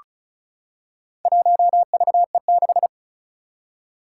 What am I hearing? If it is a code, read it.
Morse code, heard as a keyed tone, about 35 wpm, 700 Hz. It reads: 1VE6
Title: Four Challenging Letter-Number Combinations 35wpm